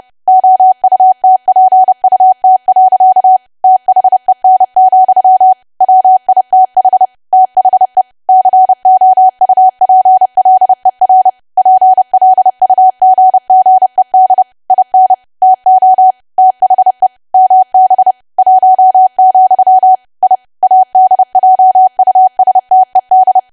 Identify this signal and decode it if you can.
Morse code, heard as a keyed tone, about 30 wpm, 750 Hz. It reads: OUTPUT. THEN, WITH THE COUPLER PLUGGED IN TO THE MB 1, I ADJUSTED
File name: cwa / W1AW-Level3 / 30WPM